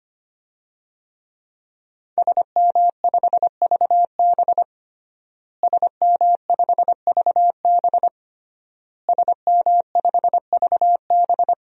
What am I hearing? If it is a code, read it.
Morse code, heard as a keyed tone, about 25 wpm, 700 Hz. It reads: SM5VB SM5VB SM5VB
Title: DXCC-100 Call Signs - 5 Characters - Repeated 3x - 25wpm